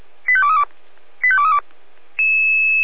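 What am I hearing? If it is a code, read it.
Five-tone selective call (ZVEI), heard as quick runs of five tones, tones of about 70 ms. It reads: 85232 85232
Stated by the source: ZVEI five-tone selective call 85855 received by SDR.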